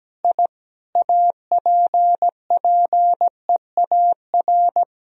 Morse code, heard as a keyed tone, about 17 wpm, 700 Hz. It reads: I APPEAR
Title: Sentences from Top 1000 Words - Review Part 2 - 17wpm